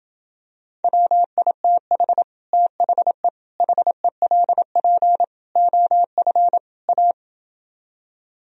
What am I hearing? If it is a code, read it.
Morse code, heard as a keyed tone, about 27 wpm, 700 Hz. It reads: WITH THE HELP OF A